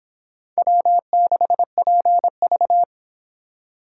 Morse code, heard as a keyed tone, about 30 wpm, 700 Hz. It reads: W6PV